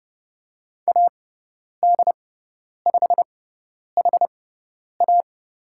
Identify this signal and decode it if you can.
Morse code, heard as a keyed tone, about 30 wpm, spaced as for 10 wpm, 700 Hz. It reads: AD5HA
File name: course / 30:10wpm / 102 Lesson 17g - Letter u - Call Signs 30wpm